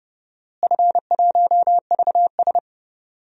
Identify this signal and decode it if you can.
Morse code, heard as a keyed tone, about 30 wpm, 700 Hz. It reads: F1VS
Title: DXCC-100 Call Signs - 4 Characters - Repeated 2x - 30wpm